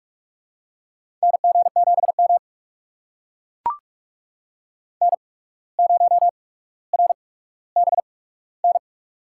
Morse code, heard as a keyed tone, about 44 wpm, spaced as for 18 wpm, 700 Hz. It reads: NG7M N0RBN
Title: USA Call Signs - 8x Character Spacing - 45wpm